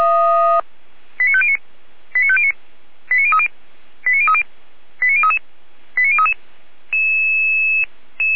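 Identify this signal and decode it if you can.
Five-tone selective call (ZVEI), heard as quick runs of five tones, tones of about 70 ms. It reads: 89509 89509 89030 89030 89033 89033
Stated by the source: ZVEI five-tone selective call received by SDR, recorded by Feuerwehr Orlamünde.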